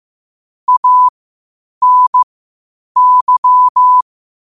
Morse code, heard as a keyed tone, about 15 wpm, spaced as for 8 wpm, 1000 Hz. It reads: ANY
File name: qotdfiles / 8wpm